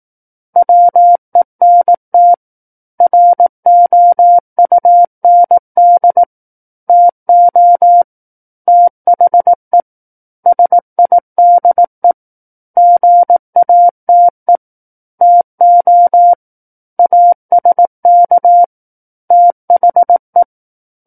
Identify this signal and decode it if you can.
Morse code, heard as a keyed tone, about 18 wpm, 700 Hz. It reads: WENT ROUND TO THE SIDE GATE TO ASK THE
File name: 18 WPM